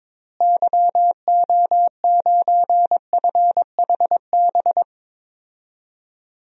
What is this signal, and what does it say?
Morse code, heard as a keyed tone, about 21 wpm, 700 Hz. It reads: YO9FHB
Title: DXCC-100 Call Signs - 6 Characters - Repeated 2x - 22wpm